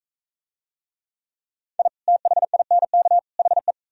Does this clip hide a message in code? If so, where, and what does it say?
Yes: Morse code from 1.8 to 3.7 s, heard as a keyed tone, about 40 wpm, 700 Hz. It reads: I THINK HE